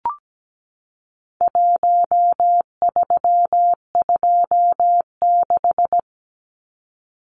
Morse code, heard as a keyed tone, about 17 wpm, 700 Hz. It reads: 1326
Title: Numbers - 4-Digit 17wpm